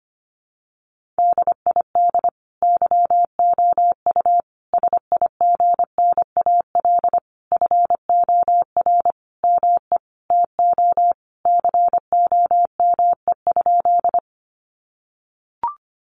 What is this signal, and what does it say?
Morse code, heard as a keyed tone, about 25 wpm, 700 Hz. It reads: DID YOU SIGNAL FOR ME TO COME?
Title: Sentences from Top 1400 Words 25wpm